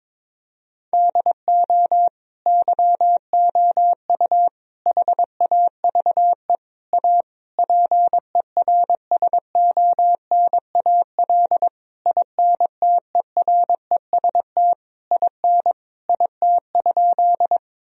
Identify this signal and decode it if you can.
Morse code, heard as a keyed tone, about 22 wpm, 700 Hz. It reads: DO YOU HAVE A PERSONAL INTEREST IN IT?
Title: Sentences from Top 1000 Words - Review Part 4 - 22wpm